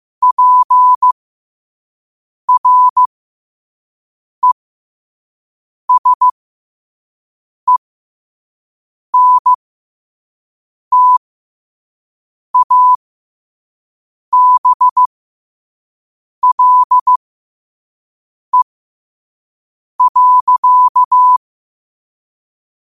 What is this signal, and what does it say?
Morse code, heard as a keyed tone, about 15 wpm, spaced as for 5 wpm, 1000 Hz. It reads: PRESENTABLE.
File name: qotdfiles / 5wpm